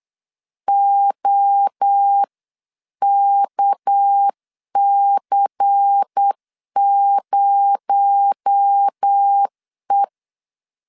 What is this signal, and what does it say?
Morse code, heard as a keyed tone, about 8 wpm, 800 Hz. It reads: OKC0E
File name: CW / DET / 1 6 1500C S C 8wpm 5219